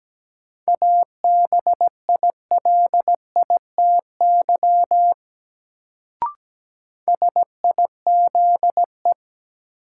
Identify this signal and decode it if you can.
Morse code, heard as a keyed tone, about 17 wpm, 700 Hz. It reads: ABILITY SIZE
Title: Sets of 3 Words - New Word Drill - 17wpm